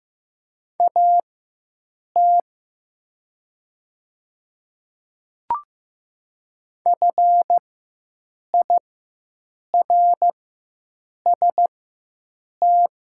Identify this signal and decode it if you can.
Morse code, heard as a keyed tone, about 15 wpm, spaced as for 7 wpm, 700 Hz. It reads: AT FIRST
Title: 4x Character Spacing - Top 300 Words - 15wpm